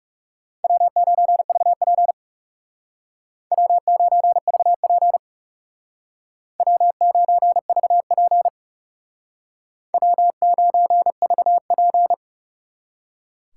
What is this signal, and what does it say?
Morse code, heard as a keyed tone, about 36 wpm, 700 Hz. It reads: W9VP W9VP W9VP W9VP
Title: Speed Racing - USA 1x2 Call Signs 30wpm